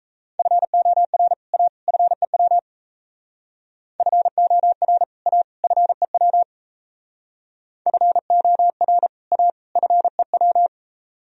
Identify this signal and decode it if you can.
Morse code, heard as a keyed tone, about 34 wpm, 700 Hz. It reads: FOR A FEW FOR A FEW FOR A FEW
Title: Speed-Racing - Sets of 3 Words 28wpm